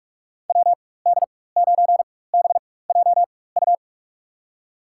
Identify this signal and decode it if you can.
Morse code, heard as a keyed tone, about 46 wpm, spaced as for 21 wpm, 700 Hz. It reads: WD9BJU
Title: USA Call Signs - 6 Characters - 4x Character Spacing - 45wpm